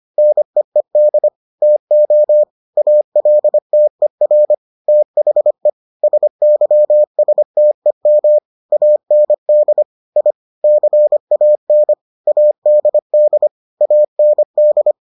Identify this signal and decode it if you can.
Morse code, heard as a keyed tone, about 25 wpm, 600 Hz. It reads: NEED TO ALTER THE SYSTEM AND I CAN ADD AND